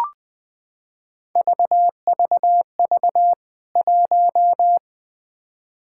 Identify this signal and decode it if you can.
Morse code, heard as a keyed tone, about 20 wpm, 700 Hz. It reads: VVV 1
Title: K1USN SST QSO Practice 20wpm